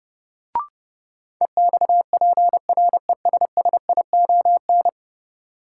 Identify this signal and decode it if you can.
Morse code, heard as a keyed tone, about 30 wpm, 700 Hz. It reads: EXPRESSION
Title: Sentences from Top 1200 Words - New Word Drill - 30wpm